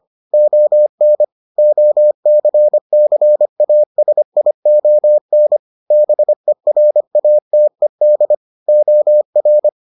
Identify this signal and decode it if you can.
Morse code, heard as a keyed tone, about 25 wpm, 600 Hz. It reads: ON OCCASION BERATED OR